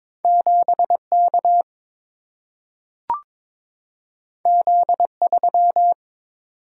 Morse code, heard as a keyed tone, about 22 wpm, 690 Hz. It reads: 7K Z3
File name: practice / 22wpm / Two Letter-Number Combinations - 22wpm